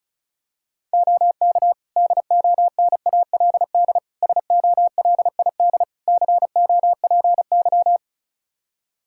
Morse code, heard as a keyed tone, about 36 wpm, 700 Hz. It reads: OK DONALD SOLID COPY
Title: Practice QSO - Exchange 3 - More Info - 35wpm